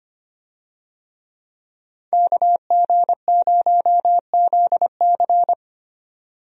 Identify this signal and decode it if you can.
Morse code, heard as a keyed tone, about 25 wpm, 700 Hz. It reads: KG0ZC
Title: USA Call Signs - 5 Characters - 25wpm